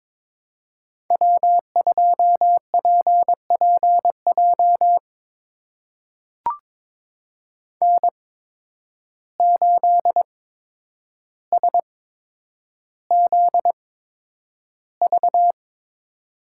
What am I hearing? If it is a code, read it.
Morse code, heard as a keyed tone, about 22 wpm, spaced as for 9 wpm, 700 Hz. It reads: W2PPJ N8SZV
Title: USA Call Signs - 5 Characters - 8x Character Spacing - 22wpm